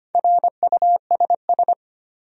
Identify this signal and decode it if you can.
Morse code, heard as a keyed tone, about 25 wpm, 700 Hz. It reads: RUSS
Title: USA Common Names - Max 4 Letters - 25wpm